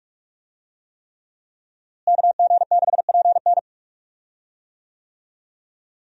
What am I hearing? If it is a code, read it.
Morse code, heard as a keyed tone, about 46 wpm, 700 Hz. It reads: KG6PN
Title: USA Call Signs - 5 Characters - Repeated 3x - 45wpm